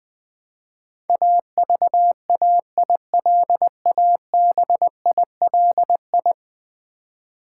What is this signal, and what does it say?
Morse code, heard as a keyed tone, about 20 wpm, 700 Hz. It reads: AVAILABILI
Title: Build A Word - 5 Letters At A Time - 12-Letter Words - 20wpm